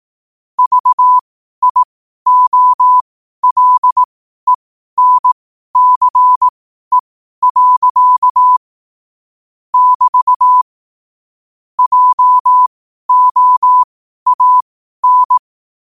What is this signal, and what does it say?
Morse code, heard as a keyed tone, about 18 wpm, spaced as for 12 wpm, 1000 Hz. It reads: VIOLENCE. <BT> JOAN
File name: qotdfiles / 12wpm